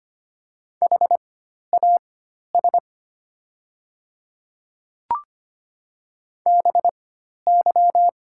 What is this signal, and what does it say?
Morse code, heard as a keyed tone, about 25 wpm, spaced as for 12 wpm, 700 Hz. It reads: HAS BY
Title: 4x Character Spacing - Top 100 Words - 25wpm